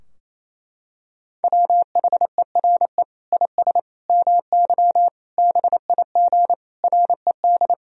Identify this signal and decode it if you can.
Morse code, heard as a keyed tone, about 28 wpm, 700 Hz. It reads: WHERE IS MY BIG RED
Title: Sentences from Top 1000 Words - Review Part 2 - 28wpm